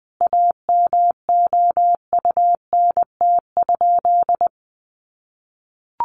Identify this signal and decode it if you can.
Morse code, heard as a keyed tone, about 20 wpm, 700 Hz. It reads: AMOUNT?
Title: Sentences from Top 700 Words 20wpm